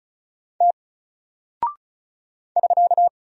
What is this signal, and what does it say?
Morse code, heard as a keyed tone, about 34 wpm, 700 Hz. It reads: T <SK>
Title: Single Character (Letter:Number:Punctuation:Prosign) 35wpm